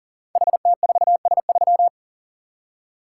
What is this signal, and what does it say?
Morse code, heard as a keyed tone, about 42 wpm, 690 Hz. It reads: HT4S3
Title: Code Groups - 5 Letter:Numbers - 40wpm